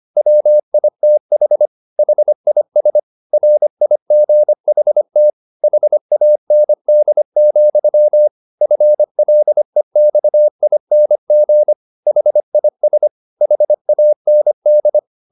Morse code, heard as a keyed tone, about 25 wpm, 600 Hz. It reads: WITH HIS RIGHT HAND, FLEXING HIS HAND